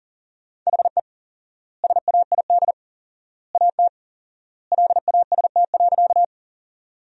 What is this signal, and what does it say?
Morse code, heard as a keyed tone, about 40 wpm, 700 Hz. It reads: HE SAID AT LAST.
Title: 4x Word-Spacing - Sentences from Top 300 Words - 40wpm